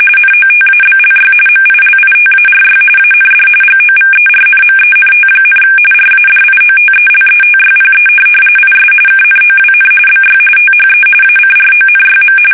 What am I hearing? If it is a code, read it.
RTTY, about 45.45 baud. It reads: NOTABLY MILITARY AND GOVERNMENT, RADIOTELETYPE IS KNOWN BY THE ACRONYM
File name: fsk850 / teletype-RTTY